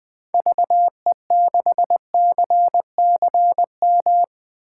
Morse code, heard as a keyed tone, about 20 wpm, 700 Hz. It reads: VE6CCM
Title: 2023 SOTA Chaser Call Signs - 20wpm